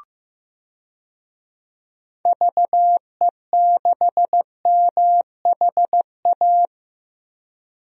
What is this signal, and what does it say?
Morse code, heard as a keyed tone, about 15 wpm, 700 Hz. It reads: VE6MHA
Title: DXCC-100 Call Signs - 6 Characters - 15wpm